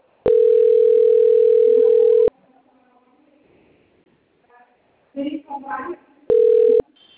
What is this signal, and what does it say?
Ringback tone (North America), 440 + 480 Hz.
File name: out- - - - - .534